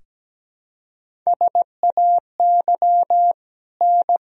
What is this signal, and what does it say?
Morse code, heard as a keyed tone, about 17 wpm, 700 Hz. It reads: SAY N
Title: Sentences from Top 200 Words and Abbreviations 17wpm